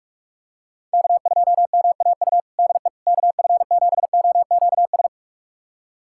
Morse code, heard as a keyed tone, about 46 wpm, 690 Hz. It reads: K2MAU DE KF7OQS
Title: USA Callsign DE Callsign 45wpm